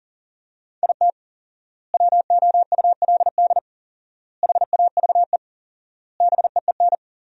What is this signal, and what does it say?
Morse code, heard as a keyed tone, about 40 wpm, 700 Hz. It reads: IT WOULD HAVE BEEN
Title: 4x Word-Spacing - Sentences from Top 400 Words - 40wpm